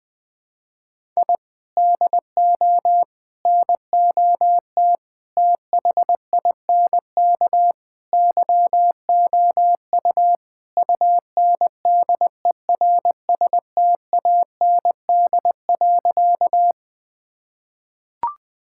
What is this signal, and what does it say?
Morse code, heard as a keyed tone, about 21 wpm, 700 Hz. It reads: I DO NOT THINK YOU UNDERSTAND.
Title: Sentences with 5-Word Phrases - 20wpm